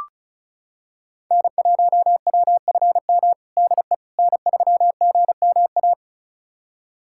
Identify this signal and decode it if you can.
Morse code, heard as a keyed tone, about 36 wpm, 700 Hz. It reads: N1WFM DE N3GMA
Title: USA Callsign DE Callsign - 5 Characters - 35wpm